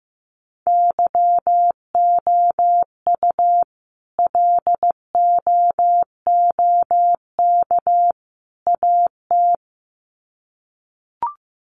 Morse code, heard as a keyed tone, about 15 wpm, 700 Hz. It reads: YOU LOOK AT
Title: Sets of 3 Words 15wpm